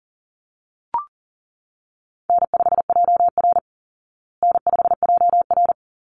Morse code, heard as a keyed tone, about 40 wpm, 700 Hz. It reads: N5JR N5JR
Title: Easy Call Signs Repeated 2x - 40wpm